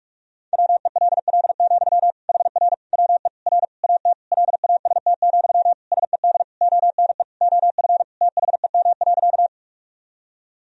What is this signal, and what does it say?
Morse code, heard as a keyed tone, about 46 wpm, 700 Hz. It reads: WELL, HR WE R AT LAST, SED ONE OF THEM.
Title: Sentences from Top 300 Words and Abbreviations 45wpm